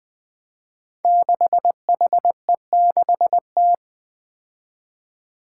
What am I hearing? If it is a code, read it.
Morse code, heard as a keyed tone, about 20 wpm, 700 Hz. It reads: 6HE6T
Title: Five Challenging Letter-Number Combinations 20wpm